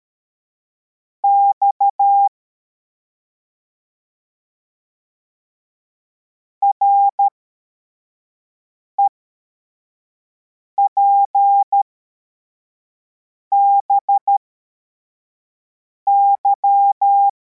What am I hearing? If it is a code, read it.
Morse code, heard as a keyed tone, about 13 wpm, spaced as for 4 wpm, 800 Hz. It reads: X REPBY